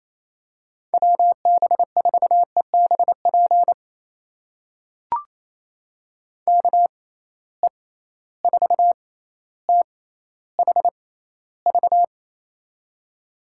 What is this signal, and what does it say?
Morse code, heard as a keyed tone, about 27 wpm, spaced as for 14 wpm, 700 Hz. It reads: WB4EBP KE4THV
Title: USA Call Signs - 6 Characters - 6x Character Spacing - 28wpm